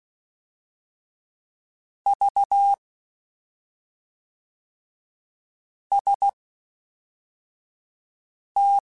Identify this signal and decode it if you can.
Morse code, heard as a keyed tone, about 16 wpm, spaced as for 3 wpm, 800 Hz. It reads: VST
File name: CW Comparisons / CW Comparisons-mp3s / CW Comparisons Broken Characters @5